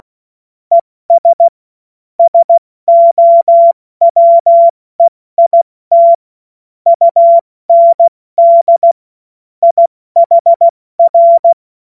Morse code, heard as a keyed tone, about 16 wpm, 680 Hz. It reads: ES SOWEIT UND IHR
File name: Morsekurs / Witze / Witz 27 Mai 2025 70 F80